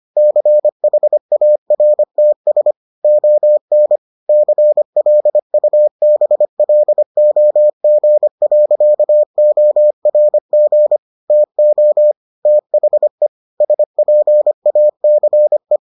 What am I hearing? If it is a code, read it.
Morse code, heard as a keyed tone, about 25 wpm, 600 Hz. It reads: CHARTS ON CLUBLOG.ORG TO THE SPACE